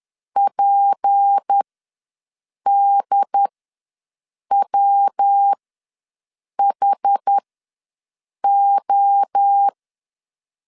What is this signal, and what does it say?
Morse code, heard as a keyed tone, about 10 wpm, spaced as for 6 wpm, 800 Hz. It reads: PDWHO
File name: LEZIONI / 6 PKX0 / 6 P ws 10wpm 3751